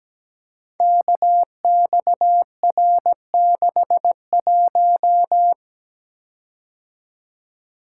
Morse code, heard as a keyed tone, about 17 wpm, 700 Hz. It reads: KXR61
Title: Code Groups - 5 Letter:Numbers - 17wpm